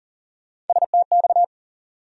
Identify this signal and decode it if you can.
Morse code, heard as a keyed tone, about 40 wpm, 700 Hz. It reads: STX
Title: USA Call Sign Postfixes - 3 Letters 40wpm